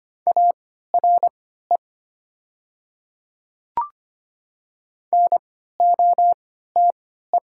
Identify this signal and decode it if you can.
Morse code, heard as a keyed tone, about 25 wpm, spaced as for 14 wpm, 700 Hz. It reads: ARE NOTE
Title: 3x Character Spacing - Top 500 Words - 25wpm